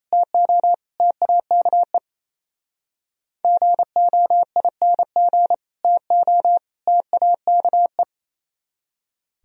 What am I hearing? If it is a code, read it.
Morse code, heard as a keyed tone, about 34 wpm, 700 Hz. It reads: TO TAKE GOING TO TAKE
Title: Speed-Racing - Sets of 3 Words 28wpm